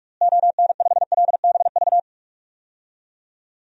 Morse code, heard as a keyed tone, about 46 wpm, 700 Hz. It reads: ON5LBV
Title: DXCC-100 Call Signs - 6 Characters - Repeated 2x - 45wpm